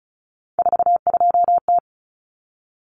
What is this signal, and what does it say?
Morse code, heard as a keyed tone, about 34 wpm, 700 Hz. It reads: 42T
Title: Common Cut Numbers - 3-Digit 35wpm